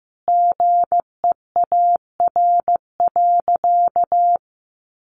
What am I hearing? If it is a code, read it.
Morse code, heard as a keyed tone, about 15 wpm, 700 Hz. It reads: GEAR.
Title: Sentences from Top 1400 Words 15wpm